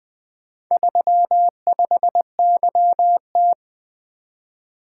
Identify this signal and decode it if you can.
Morse code, heard as a keyed tone, about 20 wpm, 700 Hz. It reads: 35YT
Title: Four Letter-Number Combinations - 20wpm